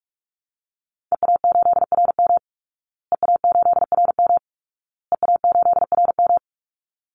Morse code, heard as a keyed tone, about 47 wpm, 700 Hz. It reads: EA8RM EA8RM EA8RM
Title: Call Signs Repeated 2x - 45wpm